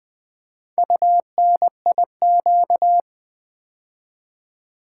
Morse code, heard as a keyed tone, about 20 wpm, 700 Hz. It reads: UNIQ
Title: Build A Word - 2 Letters At A Time - 6-Letter Words - 20wpm